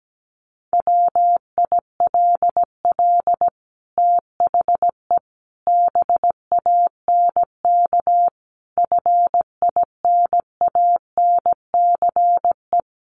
Morse code, heard as a keyed tone, about 17 wpm, 700 Hz. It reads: WILL THE BANK FINANCE
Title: Sentences from Top 1500 Words 17wpm